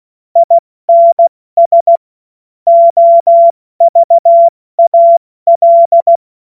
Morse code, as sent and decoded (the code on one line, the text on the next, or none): .. -. ... / --- ...- .- .-..
INS OVAL